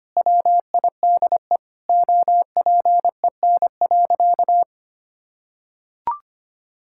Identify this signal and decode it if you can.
Morse code, heard as a keyed tone, about 25 wpm, 700 Hz. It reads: WIDE OPEN.
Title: Sentences from Top 1000 Words - Review Part 6 - 25wpm